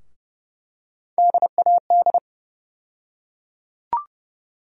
Morse code, heard as a keyed tone, about 29 wpm, 700 Hz. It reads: DAD
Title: USA Call Sign Postfixes - 3 Letters 30wpm